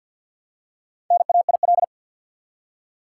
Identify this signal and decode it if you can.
Morse code, heard as a keyed tone, about 47 wpm, 700 Hz. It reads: NAIL
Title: Four Letter Words - 50wpm